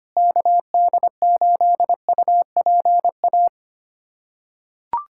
Morse code, heard as a keyed tone, about 25 wpm, 700 Hz. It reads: KD8UPA
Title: USA Callsign DE Callsign - 6 Characters - 25wpm